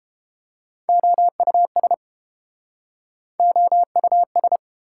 Morse code, heard as a keyed tone, about 30 wpm, 700 Hz. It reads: OUS OUS